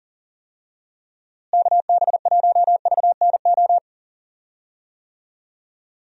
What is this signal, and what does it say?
Morse code, heard as a keyed tone, about 40 wpm, 700 Hz. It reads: KB1VNO
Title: USA Call Signs - 6 Characters - Repeated 3x - 40wpm